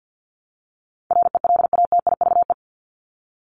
Morse code, heard as a keyed tone, about 50 wpm, 700 Hz. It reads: RELATIVE
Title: LY Suffix - Top 100 Words with Base Words 50wpm